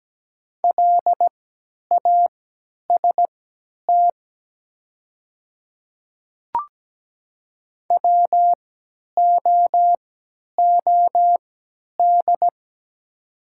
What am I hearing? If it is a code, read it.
Morse code, heard as a keyed tone, about 17 wpm, spaced as for 10 wpm, 690 Hz. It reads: LAST WOOD
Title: 3x Character Spacing - Top 300 Words - 17wpm